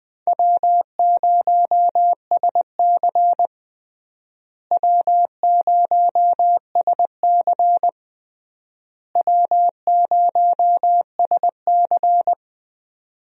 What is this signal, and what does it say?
Morse code, heard as a keyed tone, about 20 wpm, 700 Hz. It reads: W0SC W0SC W0SC
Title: USA Call Signs - 4 Characters - Repeated 3x - 20wpm